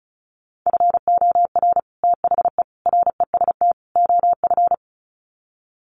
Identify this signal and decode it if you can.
Morse code, heard as a keyed tone, about 34 wpm, 700 Hz. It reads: FOR THE REST OF
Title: Sets of 4 Words 35wpm